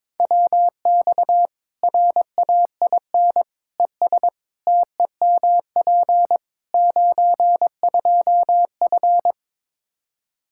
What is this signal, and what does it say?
Morse code, heard as a keyed tone, about 22 wpm, 700 Hz. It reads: WX RAIN ES TEMP 92F
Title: Practice QSO - Exchange 2 - Info - 22wpm